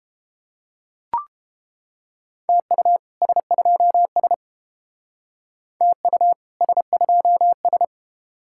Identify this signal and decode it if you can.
Morse code, heard as a keyed tone, about 29 wpm, 700 Hz. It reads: TU S2S TU S2S